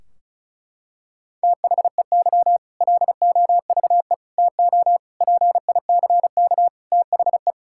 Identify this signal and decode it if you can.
Morse code, heard as a keyed tone, about 36 wpm, 700 Hz. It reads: THEY LOVE TO PICK THE
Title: Sentences from Top 1600 Words 35wpm